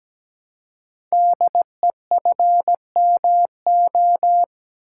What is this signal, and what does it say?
Morse code, heard as a keyed tone, about 17 wpm, 700 Hz. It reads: DEFMO
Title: Code Groups - 5 Letters - 17wpm